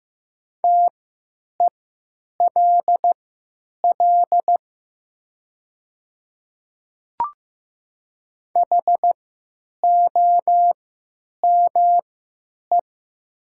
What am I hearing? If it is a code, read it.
Morse code, heard as a keyed tone, about 15 wpm, spaced as for 9 wpm, 700 Hz. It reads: TELL HOME
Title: 3x Character Spacing - Top 300 Words - 15wpm